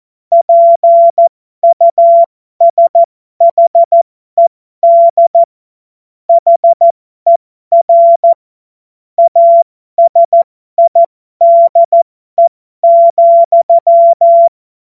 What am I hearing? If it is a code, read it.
Morse code, heard as a keyed tone, about 14 wpm, spaced as for 12 wpm, 680 Hz. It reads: PUSHED HER ASIDE,
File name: Morsekurs / Witze